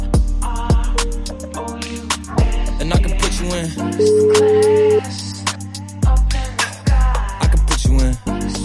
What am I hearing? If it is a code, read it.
Ringback tone (Europe, 425 Hz), with noise.